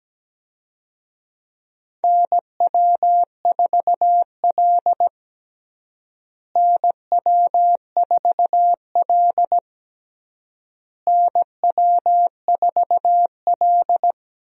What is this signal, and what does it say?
Morse code, heard as a keyed tone, about 17 wpm, 700 Hz. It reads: NW4L NW4L NW4L
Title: USA Call Signs - 4 Characters - Repeated 3x - 17wpm